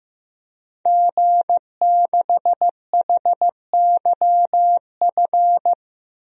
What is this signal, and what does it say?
Morse code, heard as a keyed tone, about 15 wpm, 700 Hz. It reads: G6HYF
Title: DXCC-100 Call Signs - 5 Characters - Repeated 3x - 15wpm